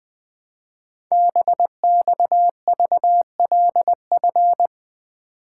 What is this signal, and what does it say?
Morse code, heard as a keyed tone, about 20 wpm, 700 Hz. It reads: BXVLF
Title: Code Groups - 5 Letters - 20wpm